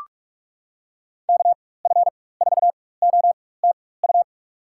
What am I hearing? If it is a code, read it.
Morse code, heard as a keyed tone, about 46 wpm, spaced as for 21 wpm, 700 Hz. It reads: KF4OTU
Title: USA Call Signs - 6 Characters - 4x Character Spacing - 45wpm